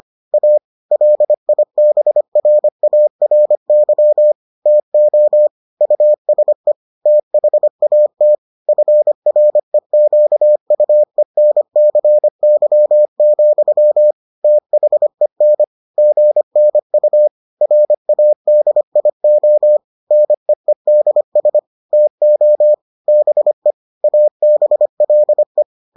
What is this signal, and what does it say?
Morse code, heard as a keyed tone, about 25 wpm, 600 Hz. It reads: A LIBRARY TO USE THAT FREQUENCY, THEN GNU RADIO NEEDS TO BE ABLE